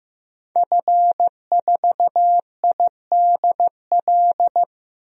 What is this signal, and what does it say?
Morse code, heard as a keyed tone, about 15 wpm, 700 Hz. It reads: F4IDL
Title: DXCC-100 Call Signs - 5 Characters - Repeated 2x - 15wpm